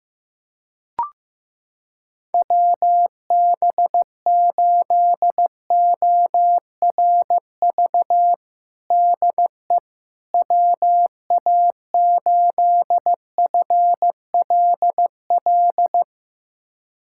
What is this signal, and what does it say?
Morse code, heard as a keyed tone, about 16 wpm, 700 Hz. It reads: WB8ORV DE WA8FLL
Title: USA Callsign DE Callsign - 6 Characters - 15wpm